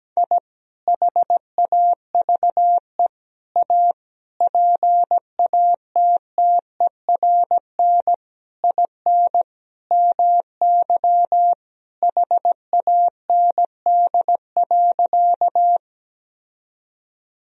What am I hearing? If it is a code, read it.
Morse code, heard as a keyed tone, about 17 wpm, 700 Hz. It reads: I HAVE A PATTERN IN MY HAND.
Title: Sentences from Top 1000 Words - Review Part 2 - 17wpm